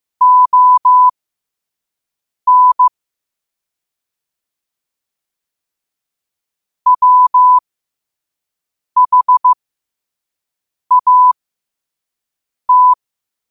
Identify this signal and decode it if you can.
Morse code, heard as a keyed tone, about 15 wpm, spaced as for 5 wpm, 1000 Hz. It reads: ON WHAT
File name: qotdfiles / 5wpm